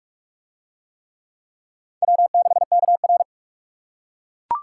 Morse code, heard as a keyed tone, about 44 wpm, 700 Hz. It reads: W6KR